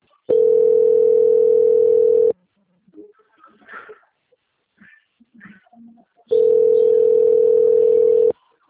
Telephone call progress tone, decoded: ringback tone

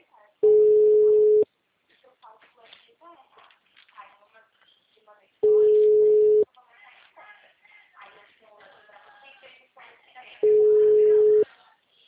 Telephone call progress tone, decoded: ringback tone

